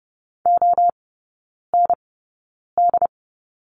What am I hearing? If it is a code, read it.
Morse code, heard as a keyed tone, about 30 wpm, spaced as for 9 wpm, 700 Hz. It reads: OND
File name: Lesson 16e - Letter c - Three Characters 30wpm